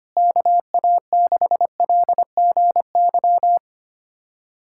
Morse code, heard as a keyed tone, about 25 wpm, 700 Hz. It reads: KA6LGY
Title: USA Callsign DE Callsign 25wpm